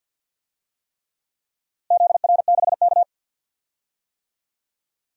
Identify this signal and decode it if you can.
Morse code, heard as a keyed tone, about 51 wpm, 700 Hz. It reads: ZR6K
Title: DXCC-100 Call Signs - 4 Characters - Repeated 3x - 50wpm